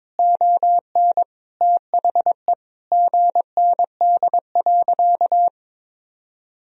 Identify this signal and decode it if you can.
Morse code, heard as a keyed tone, about 23 wpm, 700 Hz. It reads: ON THE GND.
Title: Sentences from Top 500 Words and Abbreviations 22wpm